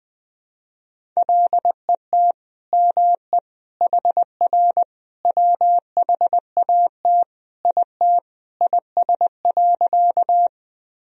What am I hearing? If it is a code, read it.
Morse code, heard as a keyed tone, about 20 wpm, 700 Hz. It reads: LET ME HR WHAT IT IS.
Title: Sentences from Top 300 Words and Abbreviations 20wpm